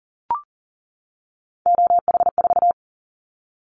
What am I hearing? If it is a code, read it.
Morse code, heard as a keyed tone, about 40 wpm, 700 Hz. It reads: OH4
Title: Callsign Prefixes for Top 50 DXCC - 40wpm